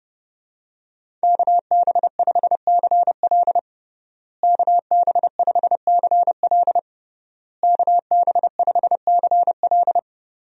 Morse code, heard as a keyed tone, about 30 wpm, 700 Hz. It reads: KB5CL KB5CL KB5CL
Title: USA Call Signs - 5 Characters - Repeated 3x - 30wpm